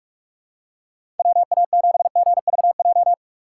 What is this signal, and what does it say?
Morse code, heard as a keyed tone, about 44 wpm, 700 Hz. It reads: WA7GVJ
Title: USA Call Signs - 6 Characters - 6x Character Spacing - 45wpm